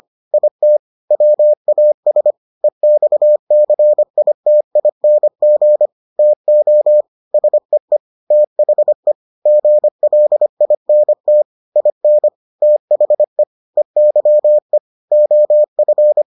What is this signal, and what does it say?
Morse code, heard as a keyed tone, about 25 wpm, 600 Hz. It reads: IT WAS EXCITING TO SEE THE GLINT IN THE EYE OF